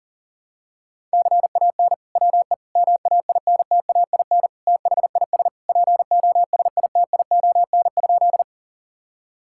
Morse code, heard as a keyed tone, about 40 wpm, 700 Hz. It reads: CAN WE MAINTAIN THIS POSITION?
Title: Sentences from Top 1000 Words - Review Part 5 - 40wpm